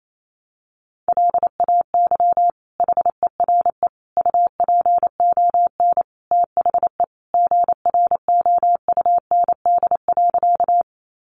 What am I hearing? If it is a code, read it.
Morse code, heard as a keyed tone, about 28 wpm, 700 Hz. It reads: LAY HERE UPON THE GROUND.
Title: Sentences from Top 1100 Words 28wpm